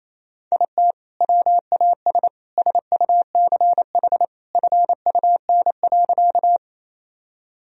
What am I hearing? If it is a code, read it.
Morse code, heard as a keyed tone, about 28 wpm, 700 Hz. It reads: IT WAS SUCH FUN.
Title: Sentences from Top 1000 Words - Review Part 4 - 28wpm